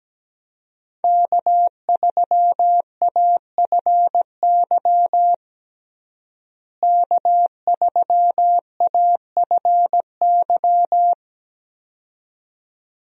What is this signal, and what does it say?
Morse code, heard as a keyed tone, about 17 wpm, 700 Hz. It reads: K3AFY K3AFY
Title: USA Call Signs - 5 Characters - Repeated 2x - 17wpm